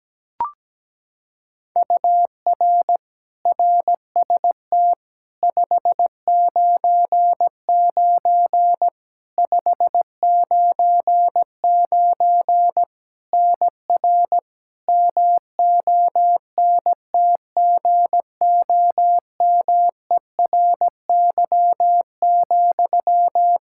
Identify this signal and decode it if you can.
Morse code, heard as a keyed tone, about 17 wpm, 700 Hz. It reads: UR RST 599 599 NR MONTGOMERY,
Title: Practice QSO - Exchange 1 - Beginning - 17wpm